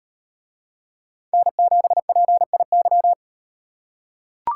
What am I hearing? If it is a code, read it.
Morse code, heard as a keyed tone, about 36 wpm, 700 Hz. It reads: N7PIY